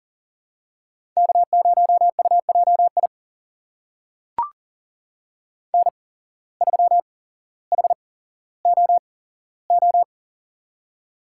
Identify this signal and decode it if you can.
Morse code, heard as a keyed tone, about 40 wpm, spaced as for 17 wpm, 700 Hz. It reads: K0UJI N3HOO
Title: USA Call Signs - 5 Characters - 8x Character Spacing - 40wpm